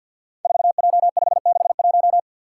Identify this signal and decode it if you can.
Morse code, heard as a keyed tone, about 51 wpm, 700 Hz. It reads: 4J561
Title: Five Challenging Letter-Number Combinations 50wpm